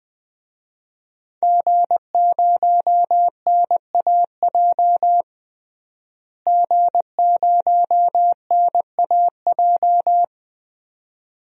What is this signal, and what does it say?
Morse code, heard as a keyed tone, about 20 wpm, 700 Hz. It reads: G0NAJ G0NAJ
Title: DXCC-100 Call Signs - 5 Characters - Repeated 2x - 20wpm